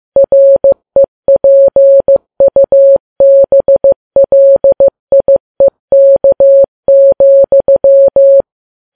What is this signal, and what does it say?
Morse code, heard as a keyed tone, about 15 wpm, 550 Hz. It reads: REPUBLIEK,
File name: Files Web tls / Morse on9fv / morse15kps